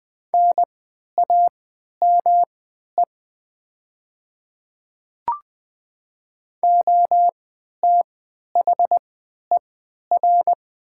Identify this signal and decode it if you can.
Morse code, heard as a keyed tone, about 20 wpm, spaced as for 11 wpm, 700 Hz. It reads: NAME OTHER
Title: 3x Character Spacing - Top 200 Words - 20wpm